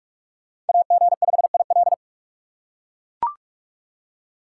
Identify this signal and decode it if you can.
Morse code, heard as a keyed tone, about 43 wpm, 700 Hz. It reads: AG5IL